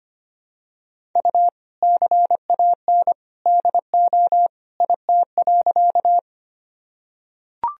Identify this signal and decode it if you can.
Morse code, heard as a keyed tone, about 25 wpm, 700 Hz. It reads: U CAN DO IT.
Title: Sentences from Top 100 Words and Abbreviations 25wpm